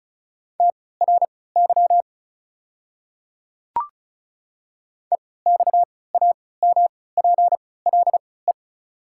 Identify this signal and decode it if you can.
Morse code, heard as a keyed tone, about 36 wpm, spaced as for 20 wpm, 690 Hz. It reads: TRY EXAMPLE
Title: 3x Character Spacing - Top 300 Words - 35wpm